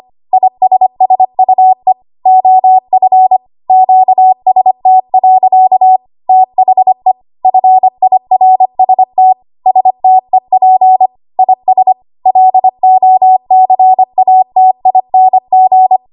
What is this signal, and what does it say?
Morse code, heard as a keyed tone, about 25 wpm, 750 Hz. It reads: ISSUE OF QST. THE FIRST STEP IS LOCATING
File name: Archive / 25 WPM / 25WPM